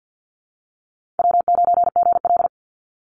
Morse code, heard as a keyed tone, about 50 wpm, 700 Hz. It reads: W9GL
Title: Call Signs Repeated 3x - 50wpm